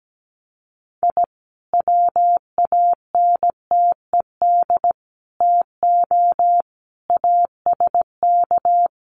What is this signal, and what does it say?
Morse code, heard as a keyed tone, about 17 wpm, 700 Hz. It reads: I WANTED TO ASK
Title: Sets of 5 Words Encore2 17wpm